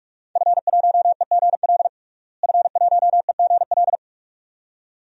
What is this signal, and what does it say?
Morse code, heard as a keyed tone, about 45 wpm, 700 Hz. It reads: F1EGL F1EGL
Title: DXCC-100 Call Signs - 5 Characters - Repeated 2x - 45wpm